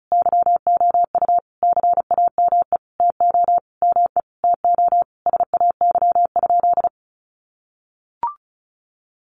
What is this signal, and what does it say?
Morse code, heard as a keyed tone, about 34 wpm, 700 Hz. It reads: YOU CAME TO ME TO SAY?
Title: Sentences from Top 200 Words 35wpm